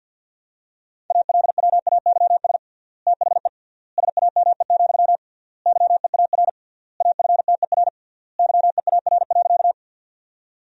Morse code, heard as a keyed tone, about 50 wpm, 700 Hz. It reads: ALWAYS THE SAME, YEAR AFTER YEAR.
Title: 3x Word-Spacing - Sentences from Top 300 Words - 50wpm